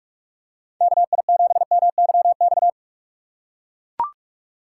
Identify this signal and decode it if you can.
Morse code, heard as a keyed tone, about 43 wpm, 700 Hz. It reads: KI7MYX
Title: USA Call Signs - 6 Characters - 3x Character Spacing - 45wpm